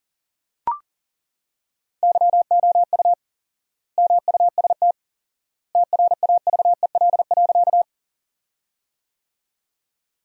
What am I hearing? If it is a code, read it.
Morse code, heard as a keyed tone, about 40 wpm, 700 Hz. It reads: YOU MUST TRAVEL.
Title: 4x Word-Spacing - Sentences from Top 400 Words - 40wpm